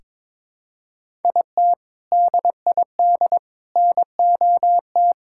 Morse code, heard as a keyed tone, about 22 wpm, 700 Hz. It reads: IT DID NOT
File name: practice / 22wpm / Sentences with 3-Word Phrases - 22wpm